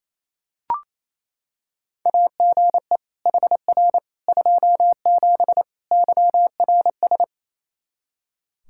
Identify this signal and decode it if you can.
Morse code, heard as a keyed tone, about 27 wpm, 700 Hz. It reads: AGE HR 27 YRS
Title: Practice QSO - Exchange 3 - More Info - 28wpm